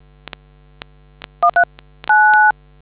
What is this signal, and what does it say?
DTMF, heard as telephone touch tones, tones of about 100 ms. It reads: 139